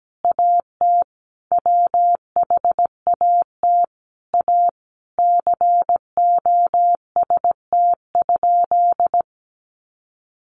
Morse code, heard as a keyed tone, about 17 wpm, 700 Hz. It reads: AT WHAT A COST?
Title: Sentences from Top 800 Words 17wpm